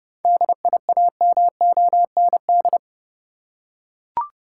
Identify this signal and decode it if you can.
Morse code, heard as a keyed tone, about 30 wpm, 700 Hz. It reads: DIAMOND
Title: Seven Letter Words - 30wpm